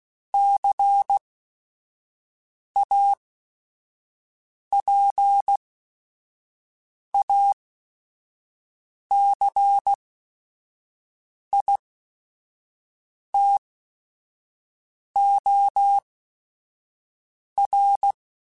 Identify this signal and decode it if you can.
Morse code, heard as a keyed tone, about 16 wpm, spaced as for 5 wpm, 800 Hz. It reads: CAPACITOR
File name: Technical Subjects / Technical Subjects-mp3s / XC @5